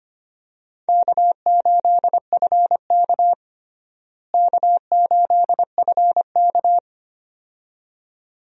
Morse code, heard as a keyed tone, about 25 wpm, 700 Hz. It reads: K8FK K8FK
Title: USA Call Signs - 4 Characters - Repeated 2x - 25wpm